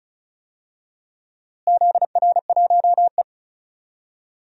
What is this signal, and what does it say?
Morse code, heard as a keyed tone, about 36 wpm, 700 Hz. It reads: ZR1E
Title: DXCC-100 Call Signs - 4 Characters - Repeated 2x - 35wpm